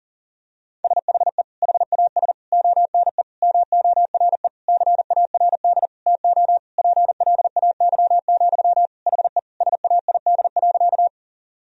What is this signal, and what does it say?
Morse code, heard as a keyed tone, about 40 wpm, 700 Hz. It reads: SHE HAS ONE MORE CARD TO PLAY, HE SAID.
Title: Sentences from Top 1000 Words - Review Part 4 - 40wpm